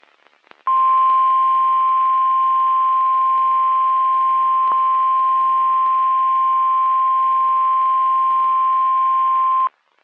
Emergency Alert System (SAME) tones (the 1050 Hz attention tone). It is heard from 0.9 to 9.6 s.
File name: eas messages / EAS WXM31 Elmira